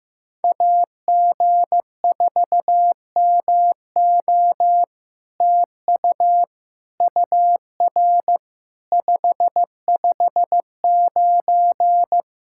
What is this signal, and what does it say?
Morse code, heard as a keyed tone, about 15 wpm, 700 Hz. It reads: AG4MO TU UR 559
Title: SOTA QSO Practice 15wpm